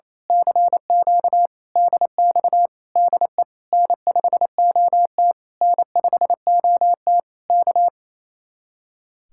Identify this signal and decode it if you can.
Morse code, heard as a keyed tone, about 27 wpm, 700 Hz. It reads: CQ DX DE N5OT N5OT K
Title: Speed-Racing Common QSO Phrases Encore - 28wpm